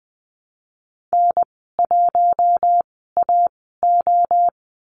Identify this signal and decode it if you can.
Morse code, heard as a keyed tone, about 20 wpm, spaced as for 14 wpm, 700 Hz. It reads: N1AO
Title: 088 Lesson 15g - Number 5 - Call Signs 20wpm